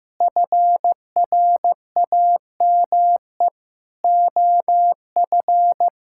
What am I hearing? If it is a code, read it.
Morse code, heard as a keyed tone, about 15 wpm, 700 Hz. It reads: FRAME OF
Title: Sentences from Top 1000 Words - Review Part 6 - 15wpm